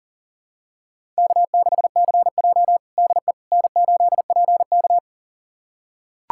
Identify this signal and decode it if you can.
Morse code, heard as a keyed tone, about 40 wpm, 700 Hz. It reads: K6CJ DE N8PK
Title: USA Callsign DE Callsign - 4 Characters - 40wpm